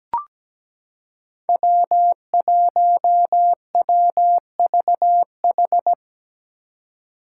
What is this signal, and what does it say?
Morse code, heard as a keyed tone, about 17 wpm, 700 Hz. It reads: W1WVH
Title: Five Challenging Letter-Number Combinations 17wpm